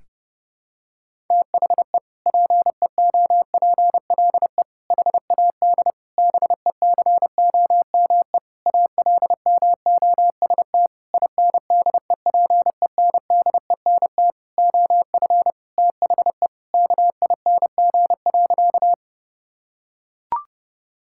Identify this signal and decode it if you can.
Morse code, heard as a keyed tone, about 28 wpm, 700 Hz. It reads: THE PEOPLE HAD BECOME ALMOST INDEPENDENT OF THE KING.
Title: Sentences from Top 1000 Words - Review Part 6 - 30wpm